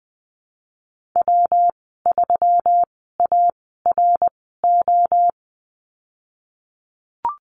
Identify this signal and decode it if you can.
Morse code, heard as a keyed tone, about 19 wpm, spaced as for 14 wpm, 700 Hz. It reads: W3ARO
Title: 140 Lesson 23g - Letter f - Call Signs 20wpm